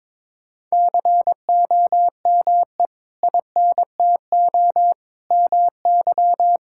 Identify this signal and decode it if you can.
Morse code, heard as a keyed tone, about 22 wpm, 700 Hz. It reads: COME INTO MY
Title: Sentences from Top 1000 Words - Review Part 6 - 22wpm